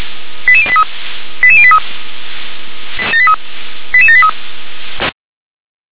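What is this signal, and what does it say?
Five-tone selective call (ZVEI), heard as quick runs of five tones, tones of about 80 ms. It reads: 88083 88083 88783
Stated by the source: ZVEI five-tone selective call received with an SDR.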